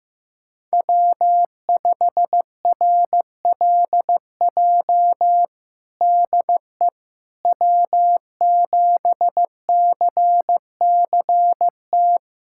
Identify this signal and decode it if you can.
Morse code, heard as a keyed tone, about 16 wpm, 700 Hz. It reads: W5RLJ DE W7CCT
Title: USA Callsign DE Callsign - 5 Characters - 15wpm